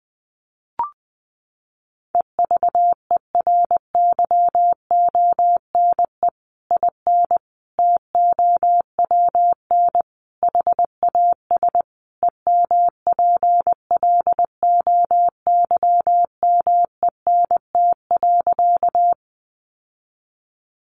Morse code, heard as a keyed tone, about 20 wpm, 700 Hz. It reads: EVERYONE IN TOWN HAS EMPLOYMENT.
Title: Sentences from Top 1500 Words 20wpm